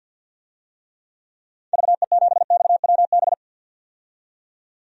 Morse code, heard as a keyed tone, about 47 wpm, 700 Hz. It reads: VE7XWB
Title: DXCC-100 Call Signs - 6 Characters - 50wpm